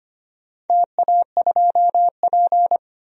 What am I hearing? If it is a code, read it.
Morse code, heard as a keyed tone, about 25 wpm, 700 Hz. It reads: TA2P
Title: DXCC-100 Call Signs - 4 Characters - 25wpm